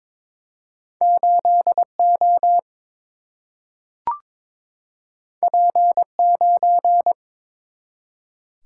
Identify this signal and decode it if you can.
Morse code, heard as a keyed tone, about 22 wpm, 700 Hz. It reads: 8 O P 9
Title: Single-Letter Single-Number 22wpm